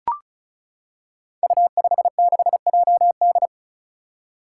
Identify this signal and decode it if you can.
Morse code, heard as a keyed tone, about 34 wpm, 700 Hz. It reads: U56JD
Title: Five Challenging Letter-Number Combinations 35wpm